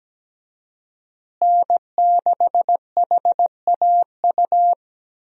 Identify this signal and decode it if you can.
Morse code, heard as a keyed tone, about 17 wpm, 700 Hz. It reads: N6HAU
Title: USA Call Signs - 5 Characters - Repeated 2x - 17wpm